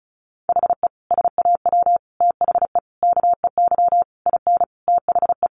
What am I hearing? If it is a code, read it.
Morse code, heard as a keyed tone, about 34 wpm, 700 Hz. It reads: HE SAW THE KEY IN THE
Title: Sentences from Top 1300 Words 35wpm